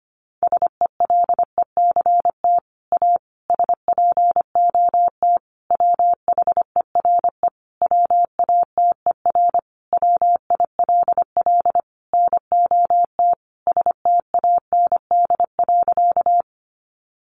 Morse code, heard as a keyed tone, about 25 wpm, 700 Hz. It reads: SELECT A SPOT WHERE WATER WILL NOT STAND.
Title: Sentences from Top 1300 Words 25wpm